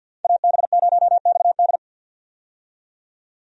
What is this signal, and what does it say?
Morse code, heard as a keyed tone, about 51 wpm, 700 Hz. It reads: AB0XD
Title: USA Call Signs - 5 Characters - 50wpm